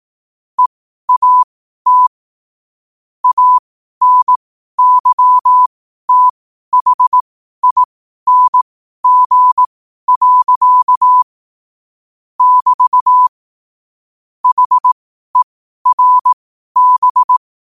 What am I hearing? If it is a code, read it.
Morse code, heard as a keyed tone, about 18 wpm, spaced as for 12 wpm, 1000 Hz. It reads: EAT ANYTHING. <BT> HERB